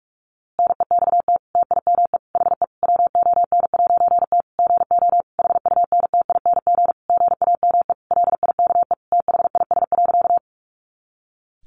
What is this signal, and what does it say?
Morse code, heard as a keyed tone, about 44 wpm, 700 Hz. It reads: NEXT TIME HE WON'T GO HUNTING GAME LIKE THIS.
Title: A Sound Of Thunder 45wpm